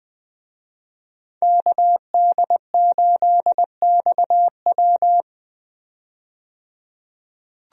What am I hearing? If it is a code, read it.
Morse code, heard as a keyed tone, about 20 wpm, 700 Hz. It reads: KD8XW
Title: USA Call Signs - 5 Characters - 20wpm